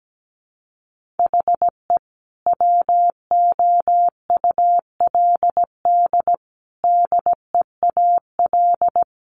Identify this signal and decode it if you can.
Morse code, heard as a keyed tone, about 17 wpm, 700 Hz. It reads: HE WOULD DEAL
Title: Sentences from Top 1400 Words 17wpm